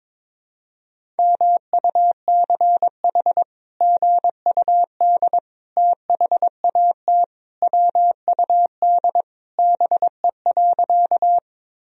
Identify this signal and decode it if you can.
Morse code, heard as a keyed tone, about 22 wpm, 700 Hz. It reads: MUCH GUD THAT WUD BE.
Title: Sentences from Top 200 Words and Abbreviations 22wpm